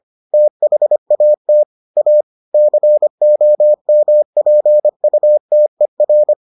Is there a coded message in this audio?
Yes: Morse code from 0.3 to 6.3 s, heard as a keyed tone, about 25 wpm, 600 Hz. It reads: THAT A COMPUTER